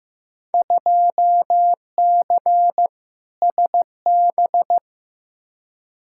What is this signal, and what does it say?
Morse code, heard as a keyed tone, about 15 wpm, 700 Hz. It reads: 2C SB
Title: Field Day Exchanges Encore - 15wpm